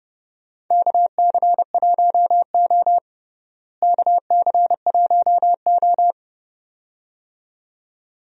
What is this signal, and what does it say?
Morse code, heard as a keyed tone, about 31 wpm, 700 Hz. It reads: KC1O KC1O
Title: USA Call Signs - 4 Characters - Repeated 2x - 30wpm